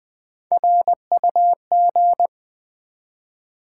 Morse code, heard as a keyed tone, about 20 wpm, 700 Hz. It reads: RUG
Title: Build A Word - 1 Letter At A Time - 5-Letter Words - 20wpm